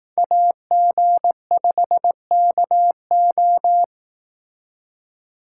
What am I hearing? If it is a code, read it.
Morse code, heard as a keyed tone, about 18 wpm, 700 Hz. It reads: AG5KO
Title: Speed Racing - USA 2x2 Call Signs 15wpm